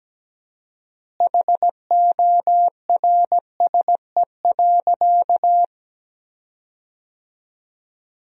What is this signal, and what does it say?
Morse code, heard as a keyed tone, about 17 wpm, 700 Hz. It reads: HORSE.
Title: 4x Word-Spacing - Sentences from Top 300 Words - 17wpm